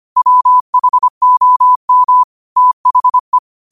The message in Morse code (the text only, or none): WHOM THE